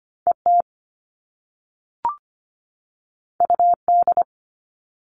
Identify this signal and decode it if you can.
Morse code, heard as a keyed tone, about 25 wpm, 700 Hz. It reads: ET UD
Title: All Cut Numbers - 2-Digit 25wpm